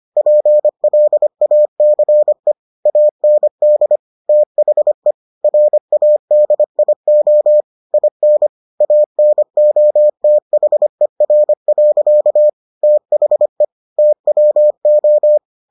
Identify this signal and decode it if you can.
Morse code, heard as a keyed tone, about 25 wpm, 600 Hz. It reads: PLACE AND THE RADIO IN ANOTHER. THE TWO